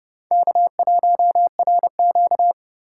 Morse code, heard as a keyed tone, about 36 wpm, 700 Hz. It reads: K1RQ
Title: Speed Racing - USA Call Signs 30wpm